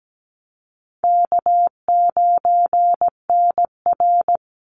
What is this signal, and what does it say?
Morse code, heard as a keyed tone, about 17 wpm, 700 Hz. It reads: K9NR
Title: Easy Call Signs - 17wpm